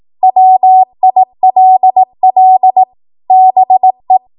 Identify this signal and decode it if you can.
Morse code, heard as a keyed tone, about 18 wpm, 750 Hz. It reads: WILL BE
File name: Morse / 18WPM